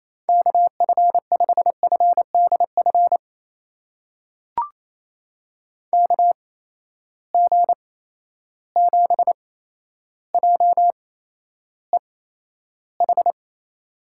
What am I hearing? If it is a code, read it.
Morse code, heard as a keyed tone, about 28 wpm, spaced as for 12 wpm, 700 Hz. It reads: KF5FDF KG7JEH
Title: USA Call Signs - 8x Character Spacing - 28wpm